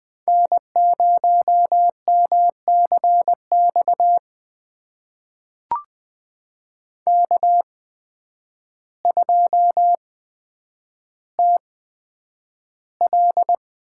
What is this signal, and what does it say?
Morse code, heard as a keyed tone, about 20 wpm, spaced as for 9 wpm, 700 Hz. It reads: N0MCX K2TL
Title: USA Call Signs - 5 Characters - 8x Character Spacing - 20wpm